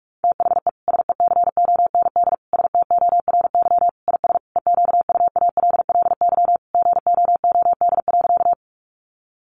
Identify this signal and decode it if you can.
Morse code, heard as a keyed tone, about 44 wpm, 700 Hz. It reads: THE SECOND STORY IS EQUALLY GOOD.
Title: Sentences from Top 1400 Words 45wpm